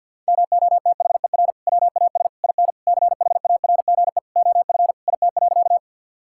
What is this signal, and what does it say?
Morse code, heard as a keyed tone, about 50 wpm, 700 Hz. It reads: MOTHER WAS IN CHARGE OF IT.
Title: Sentences from Top 1000 Words - Review Part 6 - 50wpm